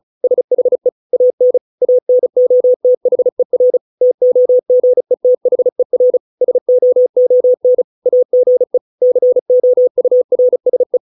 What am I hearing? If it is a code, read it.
Morse code, heard as a keyed tone, about 36 wpm, 500 Hz. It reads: SHE AN ANOTHER TOGETHER SOON AGE COURSE